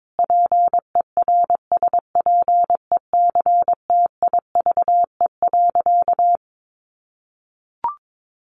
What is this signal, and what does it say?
Morse code, heard as a keyed tone, about 22 wpm, 700 Hz. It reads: PERSPECTIVE.
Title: Sentences from Top 1200 Words 22wpm